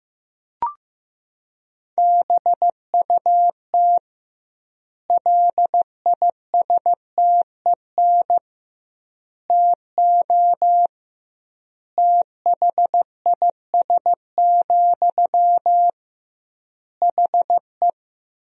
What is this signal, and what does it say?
Morse code, heard as a keyed tone, about 15 wpm, 700 Hz. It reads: BUT LISTEN TO THIS, HE
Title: 2x Word-Spacing - Sentences from Top 400 Words - 15wpm